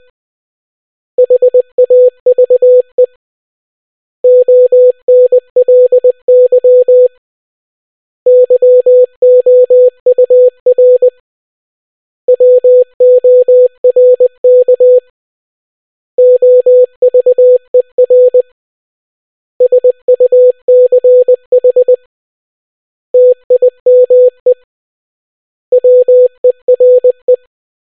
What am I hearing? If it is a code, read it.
Morse code, heard as a keyed tone, about 20 wpm, 500 Hz. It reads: HAVE ONLY YOUR WORK OVER SUCH TIME WERE